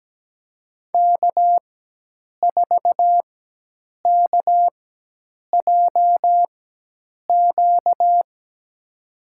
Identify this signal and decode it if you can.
Morse code, heard as a keyed tone, about 17 wpm, spaced as for 8 wpm, 700 Hz. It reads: K4KJQ
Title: USA Call Signs - 5 Characters - 4x Character Spacing - 17wpm